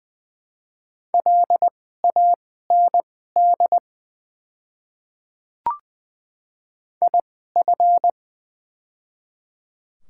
Morse code, heard as a keyed tone, about 20 wpm, spaced as for 14 wpm, 700 Hz. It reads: LAND IF
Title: 2x Character Spacing - Top 200 Words - 20wpm